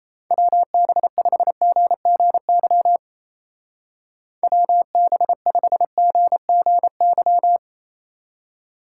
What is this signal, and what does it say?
Morse code, heard as a keyed tone, about 30 wpm, 700 Hz. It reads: WB5GGY WB5GGY
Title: Speed Racing - USA 2x3 Call Signs 28wpm